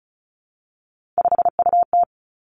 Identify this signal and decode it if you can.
Morse code, heard as a keyed tone, about 36 wpm, 700 Hz. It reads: 5UT
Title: Three Challenging Letter-Number Combinations 35wpm